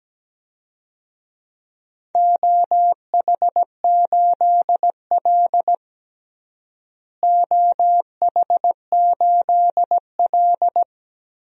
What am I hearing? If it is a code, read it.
Morse code, heard as a keyed tone, about 17 wpm, 700 Hz. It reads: OH8L OH8L
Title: DXCC-100 Call Signs - 4 Characters - Repeated 3x - 17wpm